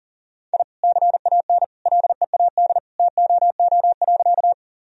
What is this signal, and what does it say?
Morse code, heard as a keyed tone, about 40 wpm, 700 Hz. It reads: I CAN LEAD TOO.
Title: Sentences from Top 1000 Words - Review Part 2 - 40wpm